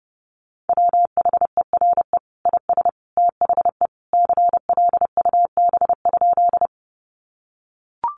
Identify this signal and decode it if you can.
Morse code, heard as a keyed tone, about 30 wpm, 700 Hz. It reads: WHERE IS THE CLUB?
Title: Sentences from Top 1200 Words 30wpm